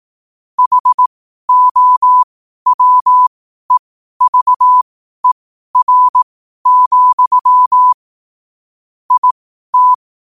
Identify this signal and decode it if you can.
Morse code, heard as a keyed tone, about 18 wpm, spaced as for 12 wpm, 1000 Hz. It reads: HOWEVER, IT
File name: qotdfiles / 12wpm